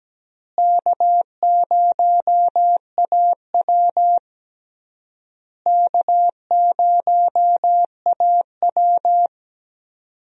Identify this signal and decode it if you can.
Morse code, heard as a keyed tone, about 17 wpm, 700 Hz. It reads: K0AW K0AW
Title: USA Call Signs - 4 Characters - Repeated 3x - 17wpm